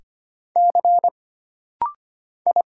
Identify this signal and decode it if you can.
Morse code, heard as a keyed tone, about 25 wpm, 700 Hz. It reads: C I
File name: Single Character (Letter:Number:Punctuation:Prosign) - Rapid-Fire - 25wpm